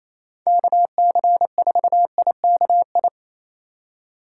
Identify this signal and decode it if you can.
Morse code, heard as a keyed tone, about 28 wpm, 700 Hz. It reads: KC4IKI
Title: USA Call Signs - 8x Character Spacing - 28wpm